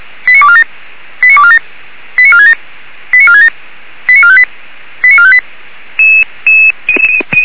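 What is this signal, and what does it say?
Five-tone selective call (ZVEI), heard as quick runs of five tones, tones of about 70 ms. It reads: 89247 89247 89467 89467 89458 89458
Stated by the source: ZVEI five-tone selective call received on SDR.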